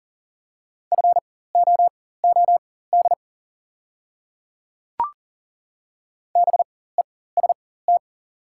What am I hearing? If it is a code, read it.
Morse code, heard as a keyed tone, about 37 wpm, spaced as for 18 wpm, 700 Hz. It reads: FOOD BEST
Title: 4x Character Spacing - Top 500 Words - 40wpm